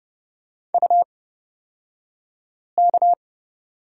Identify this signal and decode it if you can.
Morse code, heard as a keyed tone, about 30 wpm, spaced as for 5 wpm, 700 Hz. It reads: UK
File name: Lesson 36d - BT - Two Characters 30wpm